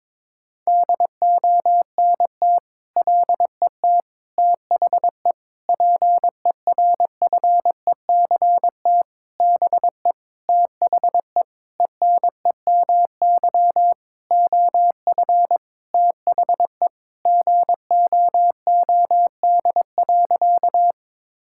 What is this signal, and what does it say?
Morse code, heard as a keyed tone, about 22 wpm, 700 Hz. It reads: DONT LET THE PERFECT BE THE ENEMY OF THE GOOD.
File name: Sentences with 9-Word Phrases - 22wpm